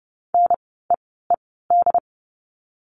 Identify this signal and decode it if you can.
Morse code, heard as a keyed tone, about 30 wpm, spaced as for 17 wpm, 700 Hz. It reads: NEED
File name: Lesson 16f - Letter c - Words 30wpm